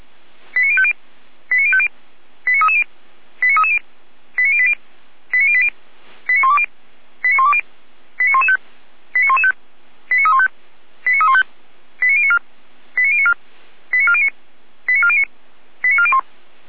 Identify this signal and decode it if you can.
Five-tone selective call (ZVEI), heard as quick runs of five tones, tones of about 70 ms. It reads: 89050 89050 89330 89330 89080 89080 89120 89120 89115 89115 89315 89315 89094 89094 89409 89409 89401